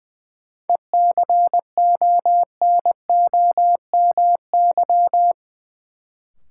Morse code, heard as a keyed tone, about 20 wpm, 700 Hz. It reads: ECONOMY
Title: Sentences from Top 1000 Words - Review Part 3 - New Word Drill - 20wpm